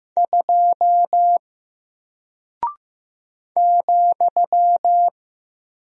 Morse code, heard as a keyed tone, about 15 wpm, 700 Hz. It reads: 2 ,
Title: Single Character (Letter:Number:Punctuation:Prosign) Alt 15wpm